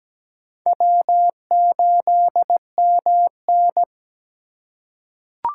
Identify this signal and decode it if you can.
Morse code, heard as a keyed tone, about 17 wpm, 700 Hz. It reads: W8MN
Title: USA Call Signs - 4 Characters - 3x Character Spacing - 17wpm